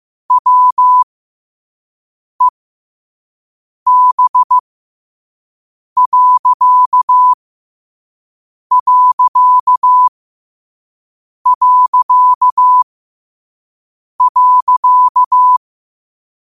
Morse code, heard as a keyed tone, about 15 wpm, spaced as for 5 wpm, 1000 Hz. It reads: WEB....
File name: qotdfiles / 5wpm